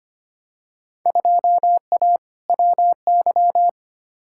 Morse code, heard as a keyed tone, about 25 wpm, 700 Hz. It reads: 2A WY
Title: Field Day Exchanges - 25wpm